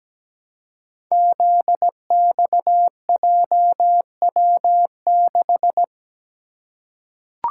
Morse code, heard as a keyed tone, about 17 wpm, 700 Hz. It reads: ZXJW6
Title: Code Groups - 5 Letter:Numbers - 17wpm